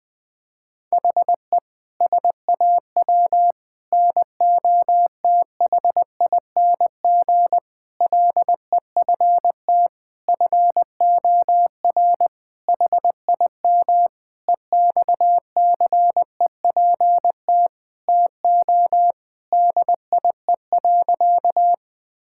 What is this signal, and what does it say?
Morse code, heard as a keyed tone, about 20 wpm, 700 Hz. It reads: HE SAW NOTHING LEFT FOR HIM EXCEPT TO DIE.
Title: Sentences from Top 1600 Words 20wpm